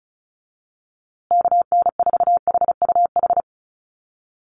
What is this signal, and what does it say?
Morse code, heard as a keyed tone, about 34 wpm, 700 Hz. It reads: KN4HUH
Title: USA Call Signs Repeated 3x 35wpm